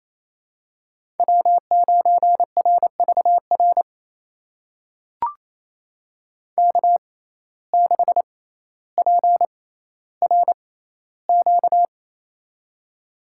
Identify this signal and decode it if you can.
Morse code, heard as a keyed tone, about 28 wpm, spaced as for 14 wpm, 700 Hz. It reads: W9RVR K6PRQ
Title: USA Call Signs - 5 Characters - 6x Character Spacing - 28wpm